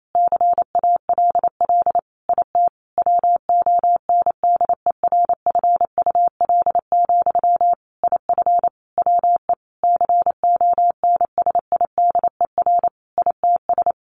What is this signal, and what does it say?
Morse code, heard as a keyed tone, about 28 wpm, 700 Hz. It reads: CALL IT WONDERFUL, IF WE CONSIDER ITS
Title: Sentences from Top 1100 Words 28wpm